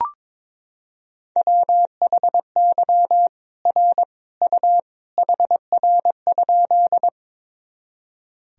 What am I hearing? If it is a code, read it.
Morse code, heard as a keyed tone, about 23 wpm, 700 Hz. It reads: WHY R U HR?
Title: Sentences from Top 200 Words and Abbreviations 22wpm